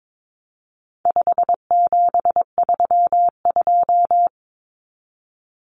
Morse code, heard as a keyed tone, about 22 wpm, 700 Hz. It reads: 5732
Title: Numbers - 4-Digit 22wpm